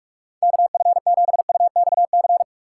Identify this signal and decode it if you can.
Morse code, heard as a keyed tone, about 46 wpm, 700 Hz. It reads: KF7UXC
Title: K1USN SST QSO Practice - 45wpm